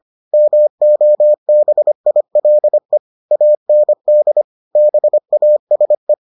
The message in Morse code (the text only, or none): MOBILE AND BASE